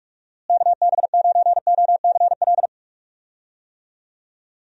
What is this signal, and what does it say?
Morse code, heard as a keyed tone, about 41 wpm, 700 Hz. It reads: KB9OCL
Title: USA Call Signs - 6 Characters - Repeated 3x - 45wpm